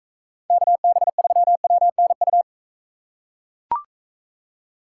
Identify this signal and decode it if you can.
Morse code, heard as a keyed tone, about 40 wpm, 700 Hz. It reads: KB3WNU